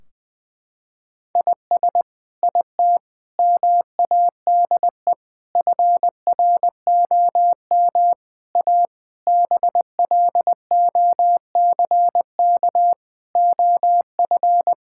Morse code, heard as a keyed tone, about 21 wpm, 700 Hz. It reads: IS IT MADE FROM A BLOCK OF
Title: Sentences from Top 1600 Words 20wpm